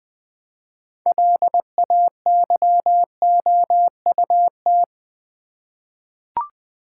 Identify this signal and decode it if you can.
Morse code, heard as a keyed tone, about 20 wpm, 700 Hz. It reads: LAYOUT
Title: 6 Letter Words - First 3 Letters Given - 20wpm